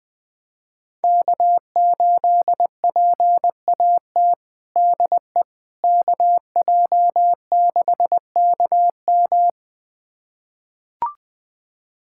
Morse code, heard as a keyed tone, about 21 wpm, 700 Hz. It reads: K8PAT DE KJ6KM
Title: USA Callsign DE Callsign - 5 Characters - 20wpm